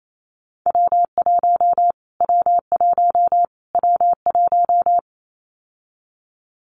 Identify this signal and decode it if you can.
Morse code, heard as a keyed tone, about 28 wpm, 700 Hz. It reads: W1 W1 W1
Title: Callsign Prefixes for Top 25 DXCC - Repeated 3x - 28wpm